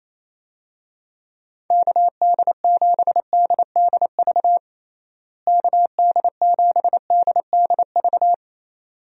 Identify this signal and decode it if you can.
Morse code, heard as a keyed tone, about 28 wpm, 700 Hz. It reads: KD7DDV KD7DDV
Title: USA Call Signs - 6 Characters - Repeated 2x - 28wpm